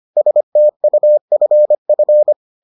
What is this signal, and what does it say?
Morse code, heard as a keyed tone, about 25 wpm, 600 Hz. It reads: STUFF